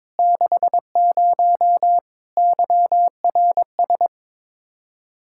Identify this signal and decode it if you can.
Morse code, heard as a keyed tone, about 22 wpm, 700 Hz. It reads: 60 YRS
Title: Practice QSO - Exchange 3 - More Info - 22wpm